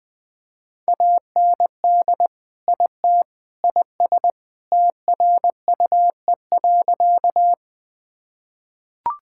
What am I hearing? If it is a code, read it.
Morse code, heard as a keyed tone, about 20 wpm, 700 Hz. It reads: AND IT IS TRUE.
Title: Sentences with 3-Word Phrases - Spoken Beginning - 20wpm